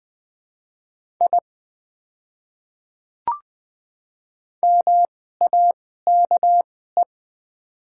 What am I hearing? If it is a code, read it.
Morse code, heard as a keyed tone, about 20 wpm, spaced as for 14 wpm, 700 Hz. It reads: I MAKE
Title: 2x Character Spacing - Top 100 Words - 20wpm